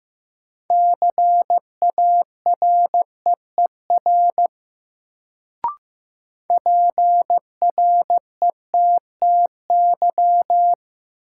Morse code, heard as a keyed tone, about 16 wpm, 700 Hz. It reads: CAREER PRETTY
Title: Sentences from Top 1000 Words - New Word Drill - 15wpm